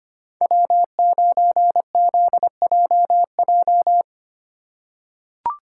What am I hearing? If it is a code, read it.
Morse code, heard as a keyed tone, about 25 wpm, 700 Hz. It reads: W9ZJJ
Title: USA Call Signs - 5 Characters - 4x Character Spacing - 25wpm